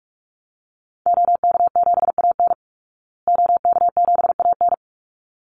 Morse code, heard as a keyed tone, about 44 wpm, 700 Hz. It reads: OK7AN OK7AN
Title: Call Signs Repeated 3x - 45wpm